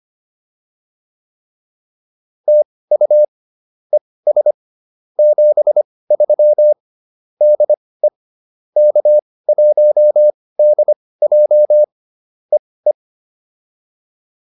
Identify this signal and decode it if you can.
Morse code, heard as a keyed tone, about 25 wpm, spaced as for 18 wpm, 600 Hz. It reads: TU ES 73 DE K1DJ EE